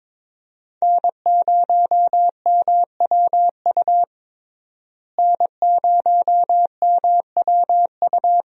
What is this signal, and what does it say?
Morse code, heard as a keyed tone, about 22 wpm, 700 Hz. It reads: N0MWU N0MWU
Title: USA Call Signs - 5 Characters - Repeated 3x - 22wpm